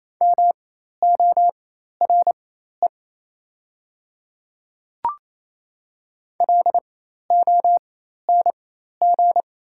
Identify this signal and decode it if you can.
Morse code, heard as a keyed tone, about 28 wpm, spaced as for 13 wpm, 690 Hz. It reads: MORE LONG
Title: 4x Character Spacing - Top 100 Words - 28wpm